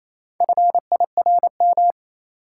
Morse code, heard as a keyed tone, about 28 wpm, 700 Hz. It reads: FIRM
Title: Sentences from Top 1400 Words - New Word Drill - 28wpm